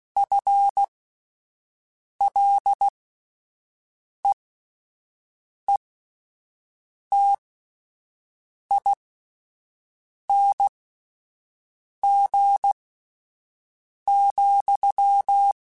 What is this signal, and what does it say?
Morse code, heard as a keyed tone, about 16 wpm, spaced as for 5 wpm, 800 Hz. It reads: FLEETING,